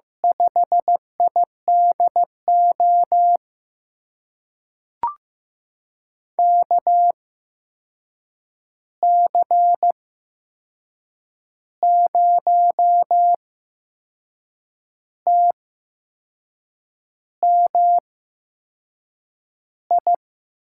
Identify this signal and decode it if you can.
Morse code, heard as a keyed tone, about 15 wpm, spaced as for 6 wpm, 700 Hz. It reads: 5IDO KC0TMI
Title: USA Call Signs - 6 Characters - 8x Character Spacing - 15wpm